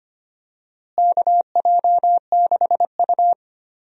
Morse code, heard as a keyed tone, about 25 wpm, 700 Hz. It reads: KJ6U
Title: USA Call Signs - 4 Characters - Repeated 3x - 25wpm